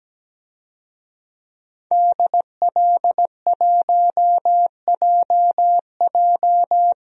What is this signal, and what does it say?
Morse code, heard as a keyed tone, about 17 wpm, 700 Hz. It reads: DL1JJ
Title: DXCC-100 Call Signs - 5 Characters - 17wpm